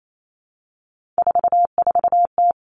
Morse code, heard as a keyed tone, about 28 wpm, 700 Hz. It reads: 44T
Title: Three Challenging Letter-Number Combinations 28wpm